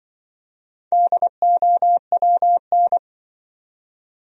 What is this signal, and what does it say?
Morse code, heard as a keyed tone, about 24 wpm, 700 Hz. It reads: DOWN
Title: Speed-Racing Top 200 Words 20wpm